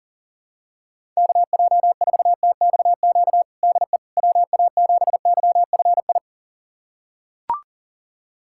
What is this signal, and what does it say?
Morse code, heard as a keyed tone, about 40 wpm, 700 Hz. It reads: KJ4TXQ DE WA7YFI
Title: USA Callsign DE Callsign - 6 Characters - 40wpm